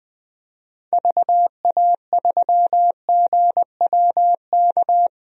Morse code, heard as a keyed tone, about 20 wpm, 700 Hz. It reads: VA3GWK
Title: DXCC-100 Call Signs - 6 Characters - 20wpm